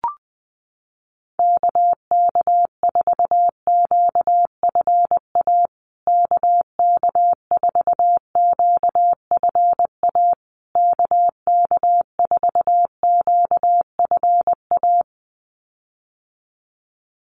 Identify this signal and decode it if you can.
Morse code, heard as a keyed tone, about 19 wpm, 700 Hz. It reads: KK4QFA KK4QFA KK4QFA
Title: USA Call Signs Repeated 3x 20wpm